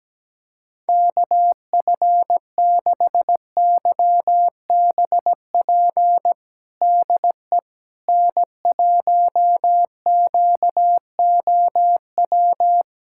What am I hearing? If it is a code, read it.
Morse code, heard as a keyed tone, about 17 wpm, 700 Hz. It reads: KF6YBP DE N1QOW
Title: USA Callsign DE Callsign 17wpm